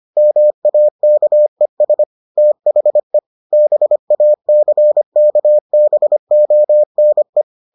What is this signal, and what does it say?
Morse code, heard as a keyed tone, about 25 wpm, 600 Hz. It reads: MAKES THE BACKBONE